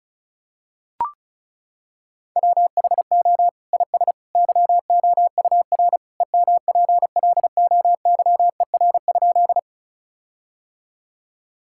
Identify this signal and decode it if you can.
Morse code, heard as a keyed tone, about 36 wpm, 700 Hz. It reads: WHO IS YOUR EMPLOYER?
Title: Sentences from Top 1600 Words 35wpm